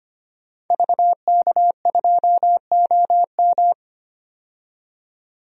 Morse code, heard as a keyed tone, about 25 wpm, 700 Hz. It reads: VK2OM
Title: DXCC-100 Call Signs - 5 Characters - 25wpm